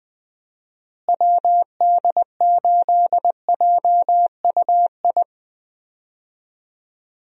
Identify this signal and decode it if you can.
Morse code, heard as a keyed tone, about 20 wpm, 700 Hz. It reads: WD8JUI
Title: USA Call Signs - 6 Characters - Repeated 3x - 20wpm